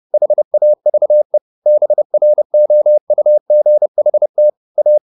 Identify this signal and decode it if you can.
Morse code, heard as a keyed tone, about 30 wpm, 600 Hz. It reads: HAVE BROUGHT A